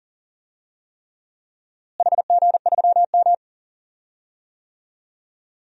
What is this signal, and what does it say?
Morse code, heard as a keyed tone, about 40 wpm, 700 Hz. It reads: HG3M
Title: DXCC-100 Call Signs - 4 Characters - Repeated 3x - 40wpm